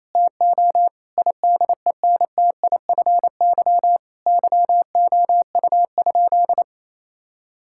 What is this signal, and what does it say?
Morse code, heard as a keyed tone, about 28 wpm, 700 Hz. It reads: TO IDENTIFY YOU?
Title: Sentences from Top 1000 Words - Review Part 6 - 28wpm